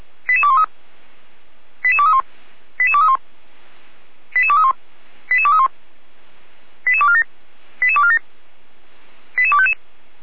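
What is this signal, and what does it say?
Five-tone selective call (ZVEI), heard as quick runs of five tones, tones of about 70 ms. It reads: 88213 88231 88231 88231 88231 88247 88247 88255
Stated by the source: ZVEI five-tone selective call received on an SDR.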